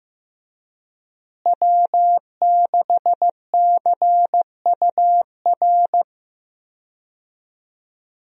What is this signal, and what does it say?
Morse code, heard as a keyed tone, about 16 wpm, 700 Hz. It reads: W6CUR
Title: USA Call Signs - 5 Characters - Repeated 2x - 15wpm